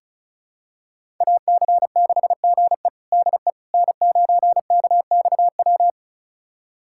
Morse code, heard as a keyed tone, about 34 wpm, 700 Hz. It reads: AC6GE DE N9KXW
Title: USA Callsign DE Callsign - 5 Characters - 35wpm